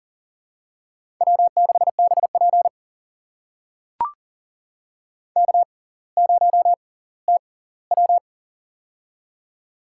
Morse code, heard as a keyed tone, about 40 wpm, spaced as for 20 wpm, 700 Hz. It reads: W6BP K0TW
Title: USA Call Signs - 4 Characters - 6x Character Spacing - 40wpm